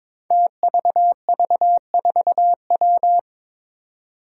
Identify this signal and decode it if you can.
Morse code, heard as a keyed tone, about 22 wpm, 700 Hz. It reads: TVV4W
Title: Five Challenging Letter-Number Combinations 22wpm